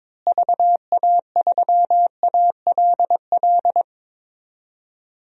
Morse code, heard as a keyed tone, about 22 wpm, 700 Hz. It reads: VA3ALL
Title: DXCC-100 Call Signs - 6 Characters - Repeated 2x - 22wpm